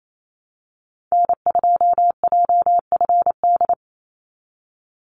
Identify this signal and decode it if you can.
Morse code, heard as a keyed tone, about 28 wpm, 700 Hz. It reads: N2JFD
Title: Call Signs Repeated 2x - 28wpm